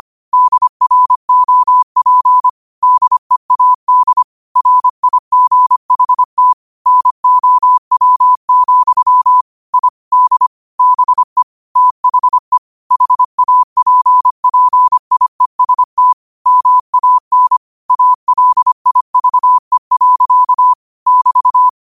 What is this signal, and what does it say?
Morse code, heard as a keyed tone, about 25 wpm, 1000 Hz. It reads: DROP DEAD RIGHT NOW, I D BE THE HAPPIEST MAN ALIVE. <BT>